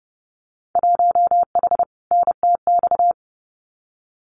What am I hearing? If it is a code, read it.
Morse code, heard as a keyed tone, about 30 wpm, 700 Hz. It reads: 1H NTX
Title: Winter Field Day Exchanges 30wpm